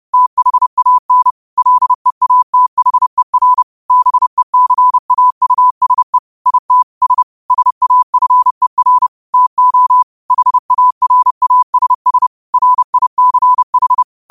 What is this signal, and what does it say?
Morse code, heard as a keyed tone, about 30 wpm, 1000 Hz. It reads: THAN LEATHER BECAUSE IT S SAFER TO HARASS RICH
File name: qotdfiles / 30wpm